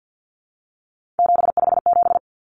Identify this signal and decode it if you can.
Morse code, heard as a keyed tone, about 50 wpm, 700 Hz. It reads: 757
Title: Common Cut Numbers - 3-Digit 50wpm